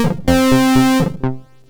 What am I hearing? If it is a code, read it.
There is no code.